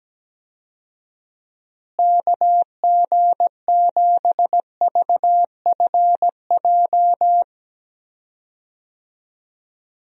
Morse code, heard as a keyed tone, about 17 wpm, 700 Hz. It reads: KG7VFJ
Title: USA Call Signs - 6 Characters - Repeated 3x - 17wpm